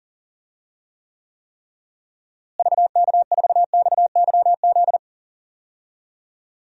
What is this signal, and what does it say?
Morse code, heard as a keyed tone, about 37 wpm, 700 Hz. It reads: VK4XYZ
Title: DXCC-100 Call Signs - 6 Characters - Repeated 3x - 40wpm